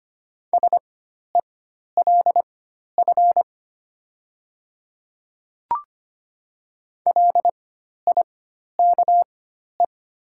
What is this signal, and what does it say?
Morse code, heard as a keyed tone, about 25 wpm, spaced as for 12 wpm, 700 Hz. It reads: SELF LIKE
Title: 4x Character Spacing - Top 300 Words - 25wpm